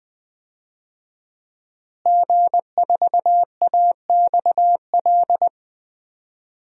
Morse code, heard as a keyed tone, about 21 wpm, 700 Hz. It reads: G4AXL
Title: DXCC-100 Call Signs - 5 Characters - Repeated 3x - 20wpm